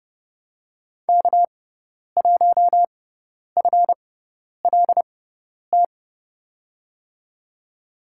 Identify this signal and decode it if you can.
Morse code, heard as a keyed tone, about 30 wpm, spaced as for 10 wpm, 690 Hz. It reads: K1FLT
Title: USA Call Signs - 5 Characters - 6x Character Spacing - 30wpm